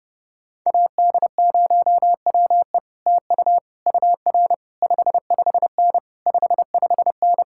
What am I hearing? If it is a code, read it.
Morse code, heard as a keyed tone, about 30 wpm, 700 Hz. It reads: AD0WE TU UR 55N 55N